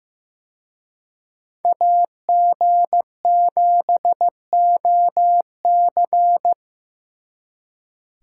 Morse code, heard as a keyed tone, about 18 wpm, 700 Hz. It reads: AG7OC
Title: Speed Racing - USA 2x2 Call Signs 15wpm